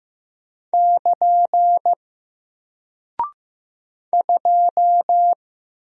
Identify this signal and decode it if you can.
Morse code, heard as a keyed tone, about 15 wpm, 700 Hz. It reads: <KN> 2
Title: Single Character (Letter:Number:Punctuation:Prosign) Alt 15wpm